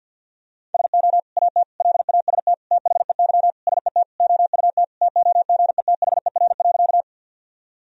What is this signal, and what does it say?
Morse code, heard as a keyed tone, about 47 wpm, 700 Hz. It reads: SO AT LAST THEY SET OUT TOGETHER.
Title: Sentences from Top 1000 Words - Review Part 1 - 50wpm